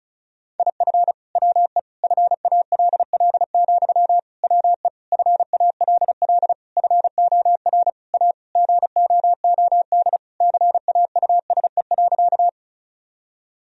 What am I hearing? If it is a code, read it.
Morse code, heard as a keyed tone, about 36 wpm, 700 Hz. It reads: IF WE FALL, WE FALL FOR A GOOD CAUSE.
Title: Sentences from Top 1000 Words - Review Part 2 - 35wpm